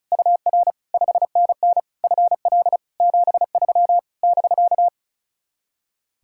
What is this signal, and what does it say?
Morse code, heard as a keyed tone, about 38 wpm, 700 Hz. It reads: UR 5NN FL 73 <BK>
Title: Speed-Racing Common QSO Phrases Encore - 35wpm